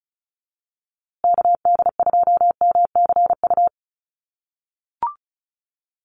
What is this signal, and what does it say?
Morse code, heard as a keyed tone, about 34 wpm, 700 Hz. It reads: KD2MCU
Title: USA Call Signs Repeated 2x 35wpm